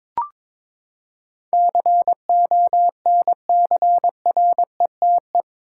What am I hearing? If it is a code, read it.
Morse code, heard as a keyed tone, about 21 wpm, 700 Hz. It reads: CONCRETE
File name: practice / 22wpm / Eight Letter Words - 22wpm